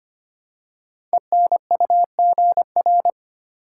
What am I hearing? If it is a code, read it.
Morse code, heard as a keyed tone, about 25 wpm, 700 Hz. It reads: ENUGR
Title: Code Groups - 5 Letters - 25wpm